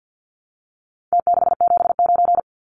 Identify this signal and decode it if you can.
Morse code, heard as a keyed tone, about 50 wpm, 700 Hz. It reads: T679
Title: Numbers - Contest Style - 4-Digit 50wpm